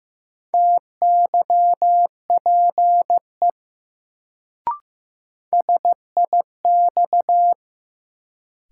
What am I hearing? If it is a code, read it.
Morse code, heard as a keyed tone, about 15 wpm, 700 Hz. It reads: TYPE SIX
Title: Sentences from Top 600 Words - New Word Drill - 15wpm